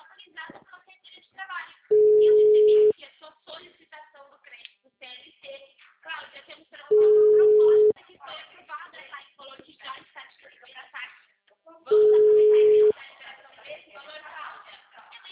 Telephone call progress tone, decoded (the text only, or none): ringback tone